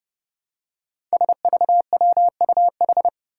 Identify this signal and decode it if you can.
Morse code, heard as a keyed tone, about 30 wpm, 700 Hz. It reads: SVWUH
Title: Five Challenging Letter-Number Combinations 30wpm